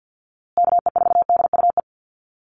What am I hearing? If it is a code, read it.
Morse code, heard as a keyed tone, about 50 wpm, 700 Hz. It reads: KE4DUE
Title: USA Call Signs Repeated 2x 50wpm